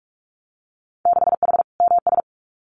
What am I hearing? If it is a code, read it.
Morse code, heard as a keyed tone, about 44 wpm, 700 Hz. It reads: 6H MS
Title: Winter Field Day Exchanges Encore 45wpm